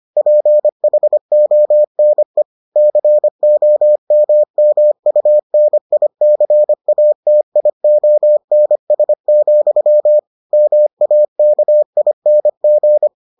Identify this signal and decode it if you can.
Morse code, heard as a keyed tone, about 25 wpm, 600 Hz. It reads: PHONE COMMUNICATIONS, MAKING